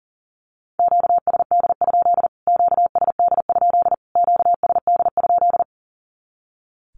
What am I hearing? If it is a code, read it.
Morse code, heard as a keyed tone, about 40 wpm, 700 Hz. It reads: QSD? QSD? QSD?
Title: Q-Codes Repeated 3x 40wpm